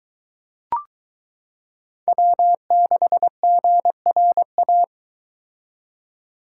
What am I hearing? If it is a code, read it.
Morse code, heard as a keyed tone, about 22 wpm, 700 Hz. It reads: W6GRA